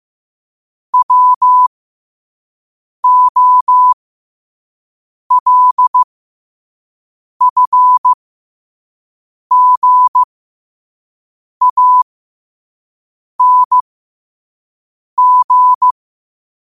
Morse code, heard as a keyed tone, about 15 wpm, spaced as for 5 wpm, 1000 Hz. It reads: WOLFGANG